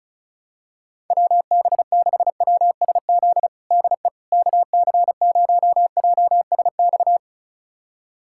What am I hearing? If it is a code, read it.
Morse code, heard as a keyed tone, about 36 wpm, 700 Hz. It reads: WB6WSZ DE KC0JSX
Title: USA Callsign DE Callsign 35wpm